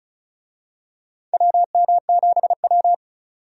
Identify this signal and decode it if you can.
Morse code, heard as a keyed tone, about 36 wpm, 700 Hz. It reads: WM7W
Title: USA Call Signs - 4 Characters - 35wpm